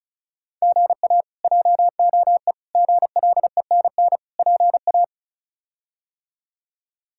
Morse code, heard as a keyed tone, about 36 wpm, 700 Hz. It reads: GA JOE GLENN PA
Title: K1USN SST QSO Practice - 35wpm